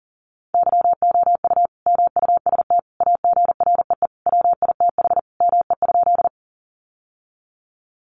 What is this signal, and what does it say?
Morse code, heard as a keyed tone, about 40 wpm, 700 Hz. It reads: YOU MUST AGREE WITH ME?
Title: Sentences from Top 1100 Words 40wpm